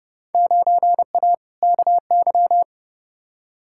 Morse code, heard as a keyed tone, about 30 wpm, 700 Hz. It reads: 9A KY
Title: Field Day Exchanges Encore - 30wpm